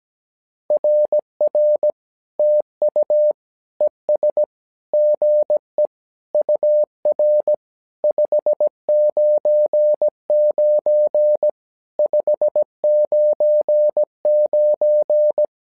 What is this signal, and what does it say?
Morse code, heard as a keyed tone, about 17 wpm, 600 Hz. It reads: RR TU ES GE UR 599 599
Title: SOTA QSO Practice 17wpm